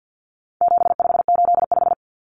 Morse code, heard as a keyed tone, about 50 wpm, 700 Hz. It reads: 7585
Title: Common Cut Numbers - 4-Digit 50wpm